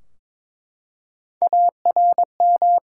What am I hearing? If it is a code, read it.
Morse code, heard as a keyed tone, about 22 wpm, 700 Hz. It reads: ARM
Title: Three Letter Words - 22wpm